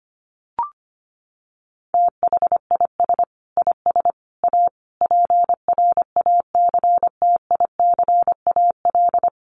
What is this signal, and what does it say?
Morse code, heard as a keyed tone, about 25 wpm, 700 Hz. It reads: THIS IS A PRACTICAL
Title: Sentences from Top 1400 Words 25wpm